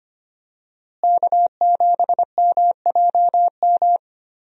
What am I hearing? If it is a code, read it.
Morse code, heard as a keyed tone, about 25 wpm, 700 Hz. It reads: K7MJM
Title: USA Call Signs - 5 Characters - Repeated 3x - 25wpm